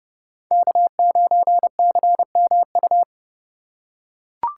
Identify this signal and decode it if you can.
Morse code, heard as a keyed tone, about 30 wpm, 700 Hz. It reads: K9CMU